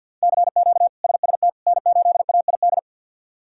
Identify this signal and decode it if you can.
Morse code, heard as a keyed tone, about 52 wpm, 700 Hz. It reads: CQ SST N8AID
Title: K1USN SST QSO Practice - 50wpm